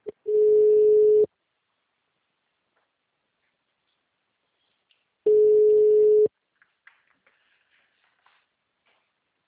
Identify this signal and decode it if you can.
Ringback tone (Europe, 425 Hz).
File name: out- -2002- - - .20